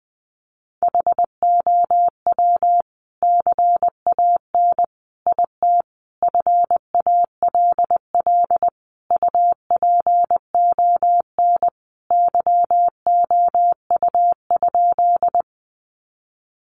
Morse code, heard as a keyed tone, about 20 wpm, 700 Hz. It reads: HOW CAN IT FALL UPON YOU?
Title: Sentences from Top 1100 Words 20wpm